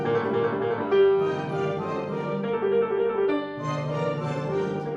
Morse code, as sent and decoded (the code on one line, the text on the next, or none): none